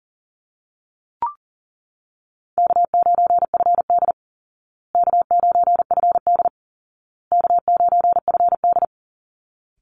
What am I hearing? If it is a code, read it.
Morse code, heard as a keyed tone, about 40 wpm, 700 Hz. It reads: K9FD K9FD K9FD
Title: Easy Call Signs Repeated 3x - 40wpm